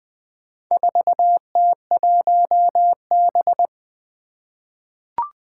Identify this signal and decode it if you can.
Morse code, heard as a keyed tone, about 20 wpm, 700 Hz. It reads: 4T1B
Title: Four Challenging Letter-Number Combinations 20wpm